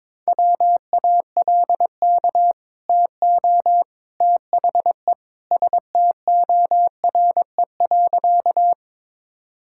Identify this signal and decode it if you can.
Morse code, heard as a keyed tone, about 22 wpm, 700 Hz. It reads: WALK TO THE STORE.
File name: practice / 22wpm / Easy Sentences with Hints - 22wpm